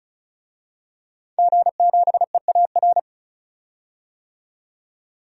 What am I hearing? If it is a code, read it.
Morse code, heard as a keyed tone, about 35 wpm, 700 Hz. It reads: G7EAR
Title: DXCC-100 Call Signs - 5 Characters - 35wpm